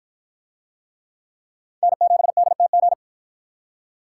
Morse code, heard as a keyed tone, about 48 wpm, 700 Hz. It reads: N7DTG